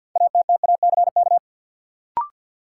Morse code, heard as a keyed tone, about 50 wpm, 700 Hz. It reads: ATTACK
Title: Sentences from Top 1500 Words - New Word Drill - 50wpm